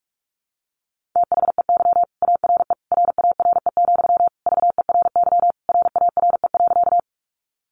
Morse code, heard as a keyed tone, about 44 wpm, 700 Hz. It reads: THEY ARE RARE, VERY RARE.
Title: Sentences from Top 1100 Words 45wpm